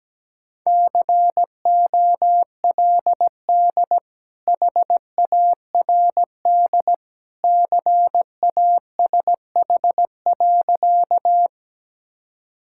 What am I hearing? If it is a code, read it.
Morse code, heard as a keyed tone, about 17 wpm, 700 Hz. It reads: COLD HARD CASH.
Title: Sentences from Top 1000 Words - Review Part 5 - 17wpm